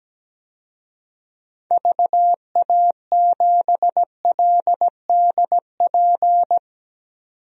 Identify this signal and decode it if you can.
Morse code, heard as a keyed tone, about 17 wpm, 700 Hz. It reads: VA7LDP
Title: DXCC-100 Call Signs - 6 Characters - 17wpm